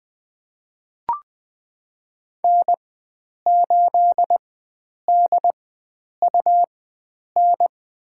Morse code, heard as a keyed tone, about 20 wpm, spaced as for 9 wpm, 700 Hz. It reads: N8DUN
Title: USA Call Signs - 4x Character Spacing - 20wpm